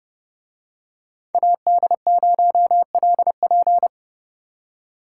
Morse code, heard as a keyed tone, about 29 wpm, 700 Hz. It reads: AD0LP
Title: Speed Racing - USA 2x2 Call Signs 30wpm